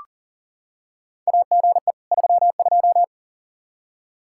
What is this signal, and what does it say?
Morse code, heard as a keyed tone, about 40 wpm, 700 Hz. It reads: AGE 32
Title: Practice QSO - Exchange 3 - More Info - 40wpm